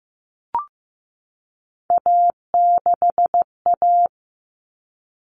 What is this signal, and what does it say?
Morse code, heard as a keyed tone, about 15 wpm, 700 Hz. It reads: A6A
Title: Common Cut Numbers - 3-Digit 15wpm